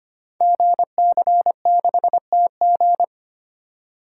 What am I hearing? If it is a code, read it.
Morse code, heard as a keyed tone, about 25 wpm, 700 Hz. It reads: GC6TG
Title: Code Groups - 5 Letter:Numbers - 25wpm